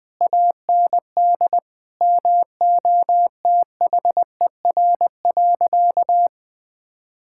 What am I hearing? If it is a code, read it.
Morse code, heard as a keyed tone, about 20 wpm, 700 Hz. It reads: AND MOTHER.
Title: Sentences from Top 1600 Words 20wpm